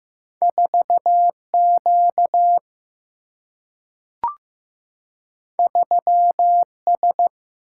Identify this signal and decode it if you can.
Morse code, heard as a keyed tone, about 15 wpm, 700 Hz. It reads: 4 Q 3 S
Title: Single-Letter Single-Number 15wpm